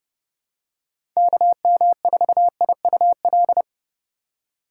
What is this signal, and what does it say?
Morse code, heard as a keyed tone, about 32 wpm, 700 Hz. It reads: KM4IUL
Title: USA Call Signs - 6 Characters - 3x Character Spacing - 30wpm